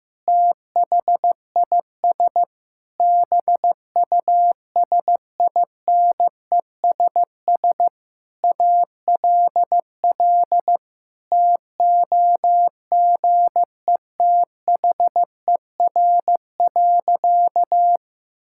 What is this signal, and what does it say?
Morse code, heard as a keyed tone, about 15 wpm, 700 Hz. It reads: THIS BUSINESS ALL TOGETHER.
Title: Sentences from Top 1000 Words - Review Part 6 - 15wpm